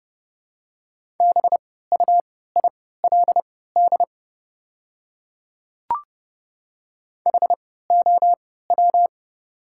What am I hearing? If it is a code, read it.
Morse code, heard as a keyed tone, about 30 wpm, spaced as for 17 wpm, 700 Hz. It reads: BUILD HOW
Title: 3x Character Spacing - Top 200 Words - 30wpm